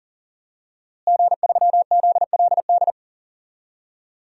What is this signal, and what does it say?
Morse code, heard as a keyed tone, about 40 wpm, 700 Hz. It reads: G3ZLD
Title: DXCC-100 Call Signs - 5 Characters - Repeated 3x - 40wpm